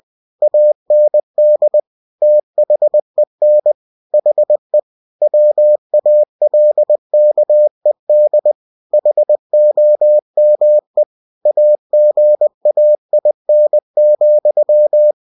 Morse code, heard as a keyed tone, about 20 wpm, 600 Hz. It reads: AND THEN HE WALKED HOME AGAIN,
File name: cw-books / winnie-the-pooh / winnie-20wpm-page-0096